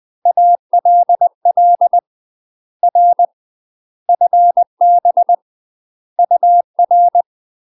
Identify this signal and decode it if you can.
Morse code, heard as a keyed tone, about 20 wpm, 700 Hz. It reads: ALL R FB UR
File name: UK MASTER QSOs TRAINING / 20WPM / UK MASTER QSOs 0012